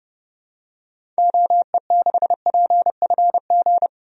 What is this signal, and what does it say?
Morse code, heard as a keyed tone, about 30 wpm, 700 Hz. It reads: OE6PFG
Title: DXCC-100 Call Signs - 6 Characters - 30wpm